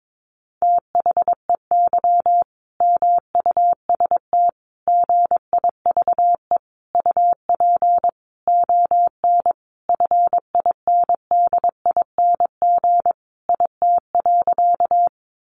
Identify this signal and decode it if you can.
Morse code, heard as a keyed tone, about 22 wpm, 700 Hz. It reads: THEY MUST GIVE UP ON FINDING IT.
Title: Sentences from Top 1100 Words 22wpm